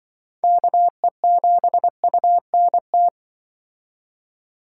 Morse code, heard as a keyed tone, about 25 wpm, 700 Hz. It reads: KE7UNT
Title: Speed Racing - USA 2x3 Call Signs 20wpm